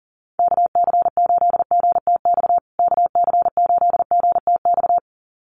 Morse code, heard as a keyed tone, about 40 wpm, 700 Hz. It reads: KC8GTX KC8GTX
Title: USA Call Signs Repeated 2x 40wpm